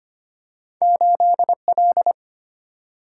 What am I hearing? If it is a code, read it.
Morse code, heard as a keyed tone, about 25 wpm, 700 Hz. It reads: 8 L
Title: Single-Letter Single-Number 25wpm